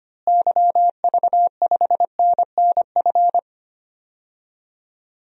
Morse code, heard as a keyed tone, about 25 wpm, 700 Hz. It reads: YV5NNF
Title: DXCC-100 Call Signs - 6 Characters - Repeated 3x - 25wpm